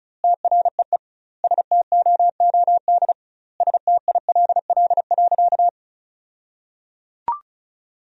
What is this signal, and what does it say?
Morse code, heard as a keyed tone, about 36 wpm, 700 Hz. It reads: TREE STOOD STILL.
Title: 2x Word-Spacing - Sentences from Top 500 Words - 35wpm